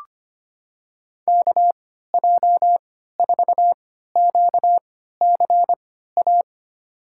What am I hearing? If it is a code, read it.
Morse code, heard as a keyed tone, about 25 wpm, spaced as for 14 wpm, 700 Hz. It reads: KJ4QCA
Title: USA Call Signs - 3x Character Spacing - 25wpm